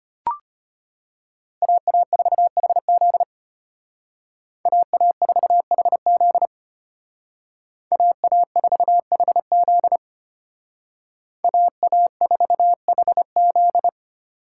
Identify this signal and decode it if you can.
Morse code, heard as a keyed tone, about 32 wpm, 700 Hz. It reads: AA4HZ AA4HZ AA4HZ AA4HZ